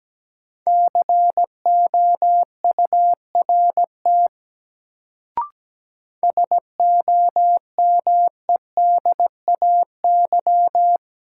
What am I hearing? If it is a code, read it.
Morse code, heard as a keyed tone, about 17 wpm, 700 Hz. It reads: COURT SOMEDAY
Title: Sentences from Top 1600 Words - New Word Drill Alt - 17wpm